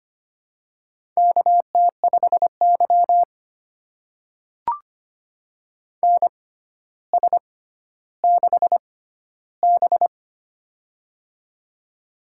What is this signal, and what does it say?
Morse code, heard as a keyed tone, about 25 wpm, spaced as for 13 wpm, 700 Hz. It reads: KT5Y NS6B
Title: USA Call Signs - 4 Characters - 6x Character Spacing - 25wpm